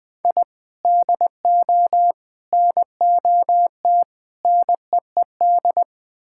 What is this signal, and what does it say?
Morse code, heard as a keyed tone, about 20 wpm, 700 Hz. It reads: I DO NOT NEED
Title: Build A Phrase - 5-Word Phrases - 20wpm